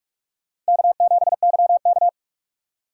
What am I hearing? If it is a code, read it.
Morse code, heard as a keyed tone, about 56 wpm, 700 Hz. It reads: K7YK